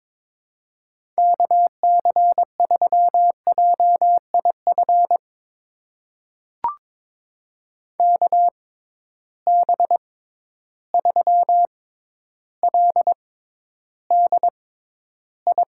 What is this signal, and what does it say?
Morse code, heard as a keyed tone, about 22 wpm, spaced as for 11 wpm, 700 Hz. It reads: KC3JIF KB3LDI
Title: USA Call Signs - 6 Characters - 6x Character Spacing - 22wpm